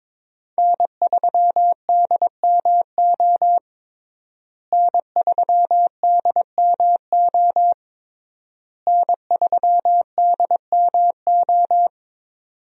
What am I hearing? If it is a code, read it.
Morse code, heard as a keyed tone, about 22 wpm, 700 Hz. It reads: N3DMO N3DMO N3DMO
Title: USA Call Signs - 5 Characters - Repeated 3x - 22wpm